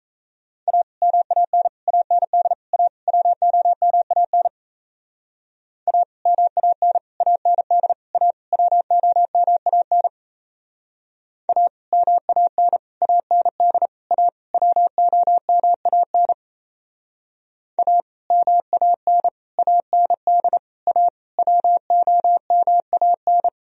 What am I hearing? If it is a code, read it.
Morse code, heard as a keyed tone, about 34 wpm, 700 Hz. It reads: A MAN AND A WOMAN A MAN AND A WOMAN A MAN AND A WOMAN A MAN AND A WOMAN
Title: Speed-Racing - Sets of 5 Words 28wpm